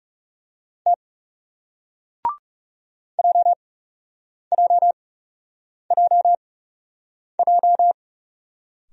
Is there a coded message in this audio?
Yes: Morse code from 0.9 to 7.9 s, heard as a keyed tone, about 41 wpm, spaced as for 8 wpm, 700 Hz. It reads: T JJJJ